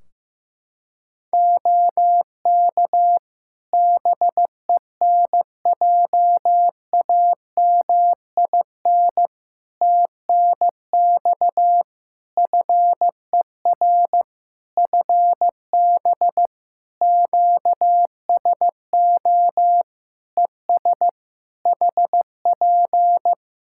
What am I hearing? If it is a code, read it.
Morse code, heard as a keyed tone, about 15 wpm, 700 Hz. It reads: OK BENJAMIN TNX FER FB QSO ES HP
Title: Practice QSO - Exchange 4 - Ending - 15wpm